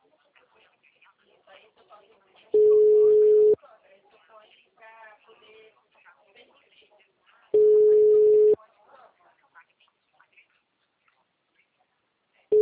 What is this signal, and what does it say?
Ringback tone (Europe, 425 Hz), with speech.